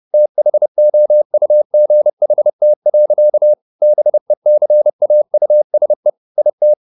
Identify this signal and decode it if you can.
Morse code, heard as a keyed tone, about 30 wpm, 600 Hz. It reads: THOUGHT. BECAUSE IT